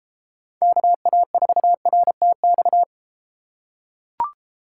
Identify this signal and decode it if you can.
Morse code, heard as a keyed tone, about 32 wpm, 700 Hz. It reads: KA4RTX